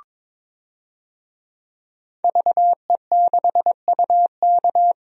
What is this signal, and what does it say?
Morse code, heard as a keyed tone, about 22 wpm, 700 Hz. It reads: VE6UK
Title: DXCC-100 Call Signs - 5 Characters - 22wpm